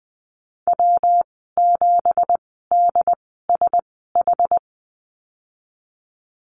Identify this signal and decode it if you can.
Morse code, heard as a keyed tone, about 20 wpm, spaced as for 15 wpm, 700 Hz. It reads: W7DSH
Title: 173 Lesson 28g - Number 9 - Call Signs 20wpm